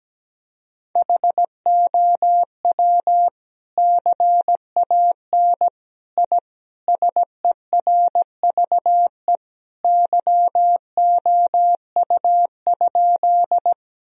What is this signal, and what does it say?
Morse code, heard as a keyed tone, about 17 wpm, 700 Hz. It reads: HOW CAN I SERVE YOU?
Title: Sentences from Top 1000 Words - Review Part 2 - 17wpm